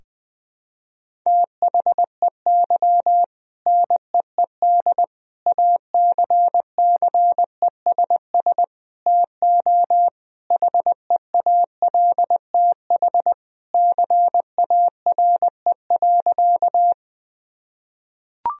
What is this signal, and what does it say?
Morse code, heard as a keyed tone, about 20 wpm, 700 Hz. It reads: THEY NEED ACCESS TO HEALTH CARE.
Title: Sentences from Top 1000 Words - Review Part 4 - 20wpm